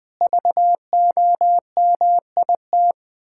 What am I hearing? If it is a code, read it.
Morse code, heard as a keyed tone, about 20 wpm, 700 Hz. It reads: VOMIT
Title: Pangram Challenge v2 - 20wpm